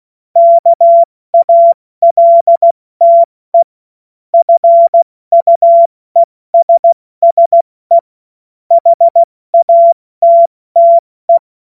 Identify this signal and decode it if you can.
Morse code, heard as a keyed tone, about 16 wpm, 680 Hz. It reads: KALTE FUESSE HATTE
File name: Morsekurs / Witze / Witz 17.Oktober 2023 70 F80